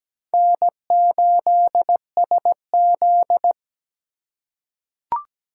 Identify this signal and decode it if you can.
Morse code, heard as a keyed tone, about 17 wpm, 700 Hz. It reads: N8SZ
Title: USA Call Signs - 4 Characters - 4x Character Spacing - 17wpm